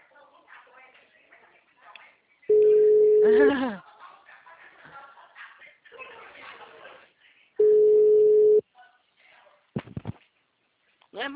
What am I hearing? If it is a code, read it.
Ringback tone (Europe, 425 Hz), with speech.
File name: out- -2010- - - .4431